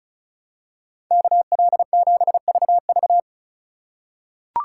Morse code, heard as a keyed tone, about 34 wpm, 700 Hz. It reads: KL7VV
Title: USA Call Signs - 5 Characters - 3x Character Spacing - 35wpm